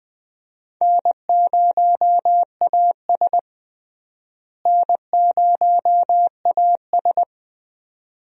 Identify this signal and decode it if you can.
Morse code, heard as a keyed tone, about 20 wpm, 700 Hz. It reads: N0AS N0AS
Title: USA Call Signs - 4 Characters - Repeated 2x - 20wpm